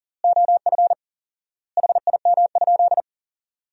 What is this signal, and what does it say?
Morse code, heard as a keyed tone, about 40 wpm, 700 Hz. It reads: OF HIM?
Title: 4x Word-Spacing - Sentences from Top 100 Words - 40wpm